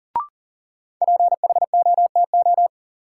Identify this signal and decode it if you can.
Morse code, heard as a keyed tone, about 40 wpm, 700 Hz. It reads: PHOTO
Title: Sentences from Top 1200 Words - New Word Drill - 40wpm